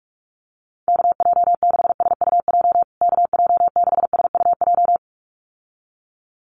Morse code, heard as a keyed tone, about 44 wpm, 700 Hz. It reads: KJ6SUJ KJ6SUJ
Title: USA Call Signs Repeated 2x 45wpm